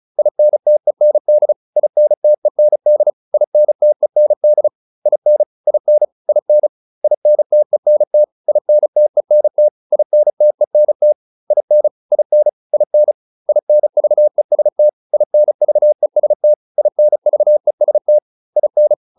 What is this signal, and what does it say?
Morse code, heard as a keyed tone, about 34 wpm, 600 Hz. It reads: INTEND INTEND INTEND IN IN IN INTENT INTENT INTENT IN IN IN INVEST INVEST INVEST IN
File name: IN6 35